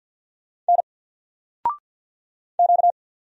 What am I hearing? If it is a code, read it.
Morse code, heard as a keyed tone, about 52 wpm, 700 Hz. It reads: N Q
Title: Single Letter-Number Alt 50wpm